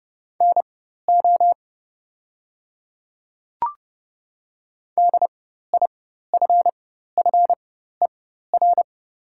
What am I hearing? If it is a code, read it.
Morse code, heard as a keyed tone, about 30 wpm, spaced as for 14 wpm, 700 Hz. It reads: NO DIFFER
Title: 4x Character Spacing - Top 200 Words - 30wpm